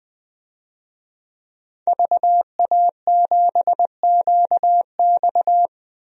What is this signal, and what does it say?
Morse code, heard as a keyed tone, about 20 wpm, 700 Hz. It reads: VA7QX
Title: DXCC-100 Call Signs - 5 Characters - 20wpm